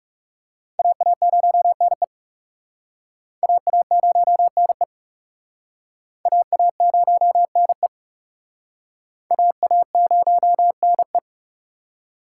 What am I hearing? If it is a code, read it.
Morse code, heard as a keyed tone, about 37 wpm, 700 Hz. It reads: AA0NE AA0NE AA0NE AA0NE
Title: Speed Racing - USA 2x2 Call Signs 30wpm